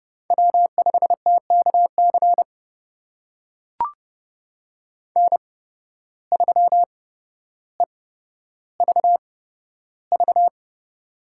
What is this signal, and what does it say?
Morse code, heard as a keyed tone, about 30 wpm, spaced as for 13 wpm, 700 Hz. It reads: W5TKC N3EVV
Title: USA Call Signs - 5 Characters - 8x Character Spacing - 30wpm